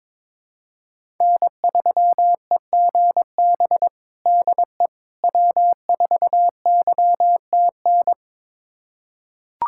Morse code, heard as a keyed tone, about 22 wpm, 700 Hz. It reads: N3EGB DE W4YTN
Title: USA Callsign DE Callsign - 5 Characters - 22wpm